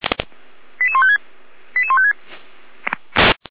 Five-tone selective call (ZVEI), heard as quick runs of five tones, tones of about 80 ms. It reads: 88156 88156